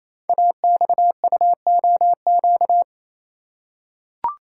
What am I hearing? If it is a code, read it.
Morse code, heard as a keyed tone, about 28 wpm, 700 Hz. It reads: AXUOQ
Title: Code Groups - 5 Letters - 28wpm